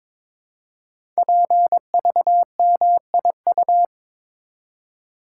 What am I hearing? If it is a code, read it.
Morse code, heard as a keyed tone, about 22 wpm, 700 Hz. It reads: PVMIU
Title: Code Groups - 5 Letter:Numbers - 22wpm